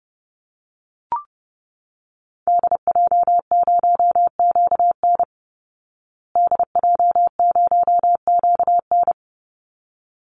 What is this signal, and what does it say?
Morse code, heard as a keyed tone, about 30 wpm, 700 Hz. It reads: DJ0QN DJ0QN
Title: Call Signs Repeated 3x - 30wpm